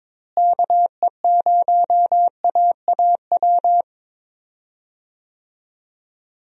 Morse code, heard as a keyed tone, about 22 wpm, 700 Hz. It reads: KE0AAW
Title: USA Call Signs - 6 Characters - Repeated 2x - 22wpm